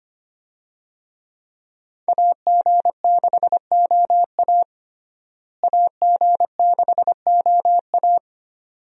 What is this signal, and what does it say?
Morse code, heard as a keyed tone, about 25 wpm, 700 Hz. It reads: AG6OA AG6OA
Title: USA Call Signs - 5 Characters - Repeated 2x - 25wpm